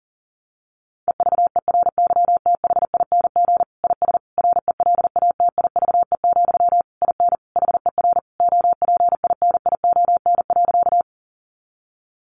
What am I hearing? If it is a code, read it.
Morse code, heard as a keyed tone, about 40 wpm, 700 Hz. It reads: EVERYTHING IS RELATIVE, IN HER OPINION.
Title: Sentences from Top 1400 Words 40wpm